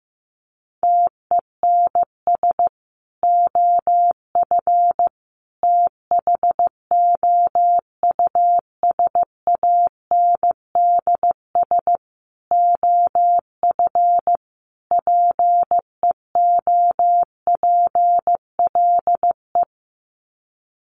Morse code, heard as a keyed tone, about 15 wpm, 700 Hz. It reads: TENS OF THOUSANDS OF PEOPLE
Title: Sets of 5 Words Encore2 15wpm